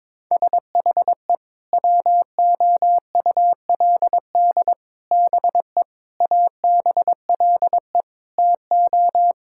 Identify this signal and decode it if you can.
Morse code, heard as a keyed tone, about 22 wpm, 700 Hz. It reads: SHE WOULD BE ABLE TO
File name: practice / 22wpm / Sentences with 4-Word Phrases - 22wpm